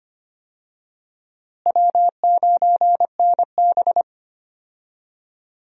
Morse code, heard as a keyed tone, about 25 wpm, 700 Hz. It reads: W9NB
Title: USA Call Signs - 4 Characters - Repeated 2x - 25wpm